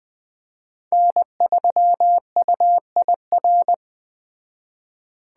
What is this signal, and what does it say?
Morse code, heard as a keyed tone, about 20 wpm, 700 Hz. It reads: N3UIR
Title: Speed Racing - USA Call Signs 17wpm